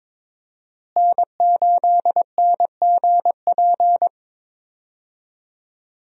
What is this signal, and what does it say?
Morse code, heard as a keyed tone, about 22 wpm, 700 Hz. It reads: N8NGP
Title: USA Call Signs - 5 Characters - Repeated 2x - 22wpm